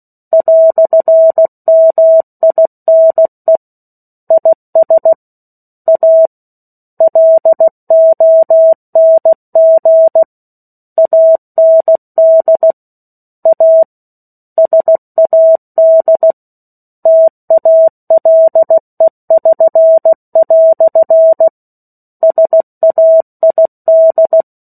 Morse code, heard as a keyed tone, about 16 wpm, 650 Hz. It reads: "MINE IS A LONG AND A SAD TALE<SN>" SAID
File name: 03-Chapter-03 16 WPM